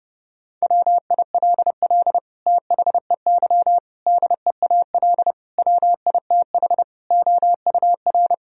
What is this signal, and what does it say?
Morse code, heard as a keyed tone, about 30 wpm, 700 Hz. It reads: WILL THEY DEAL WITH OUR
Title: Sentences from Top 1000 Words - Review Part 4 - 30wpm